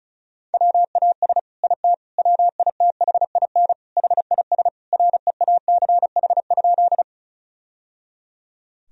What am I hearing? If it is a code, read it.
Morse code, heard as a keyed tone, about 36 wpm, 700 Hz. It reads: WAS IT WITHIN HIS REACH?
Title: Sentences from Top 1000 Words - Review Part 4 - 35wpm